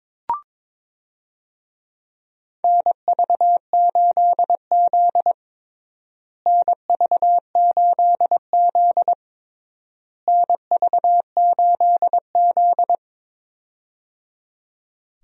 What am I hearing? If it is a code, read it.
Morse code, heard as a keyed tone, about 22 wpm, 700 Hz. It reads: NV8Z NV8Z NV8Z
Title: USA Call Signs - 4 Characters - Repeated 3x - 22wpm